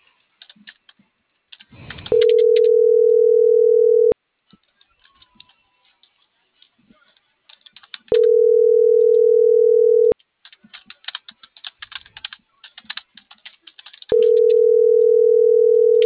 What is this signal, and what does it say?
Ringback tone (North America), 440 + 480 Hz.